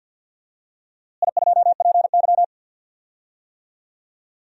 Morse code, heard as a keyed tone, about 50 wpm, 700 Hz. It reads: I2PY
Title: DXCC-100 Call Signs - 4 Characters - Repeated 3x - 50wpm